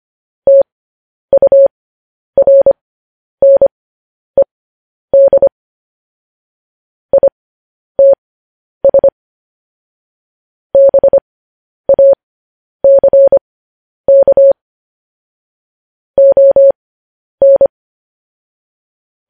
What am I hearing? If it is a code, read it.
Morse code, heard as a keyed tone, about 25 wpm, spaced as for 10 wpm, 550 Hz. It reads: TURNED ITS BACK ON
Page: news-headlines-in-morse-code-at-10-wpm